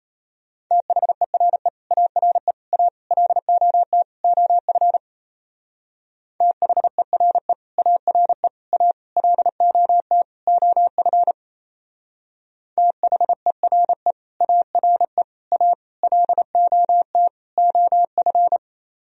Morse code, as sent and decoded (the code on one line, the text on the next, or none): - .... .-. / .- .-. / .- / .-.. --- - / --- ..-. / - .... .-. / .- .-. / .- / .-.. --- - / --- ..-. / - .... .-. / .- .-. / .- / .-.. --- - / --- ..-.
THERE ARE A LOT OF THERE ARE A LOT OF THERE ARE A LOT OF